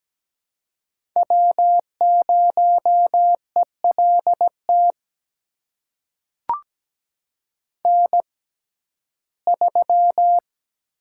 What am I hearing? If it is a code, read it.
Morse code, heard as a keyed tone, about 17 wpm, spaced as for 12 wpm, 700 Hz. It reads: W0ELT N3
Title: USA Call Signs - 5 Characters - 6x Character Spacing - 17wpm